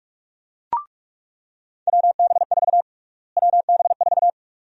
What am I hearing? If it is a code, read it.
Morse code, heard as a keyed tone, about 43 wpm, 700 Hz. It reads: WB4 WB4
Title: USA Call Sign Prefixes - Repeated 2x - 45wpm